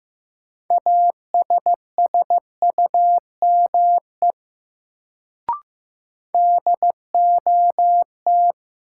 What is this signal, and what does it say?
Morse code, heard as a keyed tone, about 15 wpm, 700 Hz. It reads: ASSUME DOT
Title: Sentences from Top 1500 Words - New Word Drill Alt - 15wpm